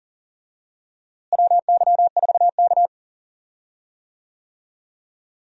Morse code, heard as a keyed tone, about 40 wpm, 700 Hz. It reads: WY4K
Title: USA Call Signs - 4 Characters - Repeated 2x - 40wpm